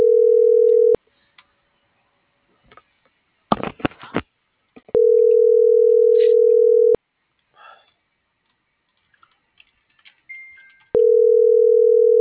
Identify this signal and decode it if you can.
Ringback tone (North America), 440 + 480 Hz.